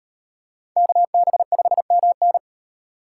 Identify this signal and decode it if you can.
Morse code, heard as a keyed tone, about 38 wpm, 700 Hz. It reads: KB5MN